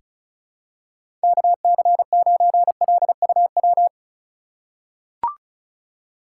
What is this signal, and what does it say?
Morse code, heard as a keyed tone, about 36 wpm, 700 Hz. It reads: KC9LUW
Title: USA Call Signs - 6 Characters - 4x Character Spacing - 35wpm